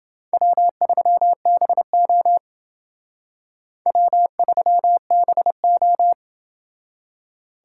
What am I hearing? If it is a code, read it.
Morse code, heard as a keyed tone, about 30 wpm, 700 Hz. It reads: W3BO W3BO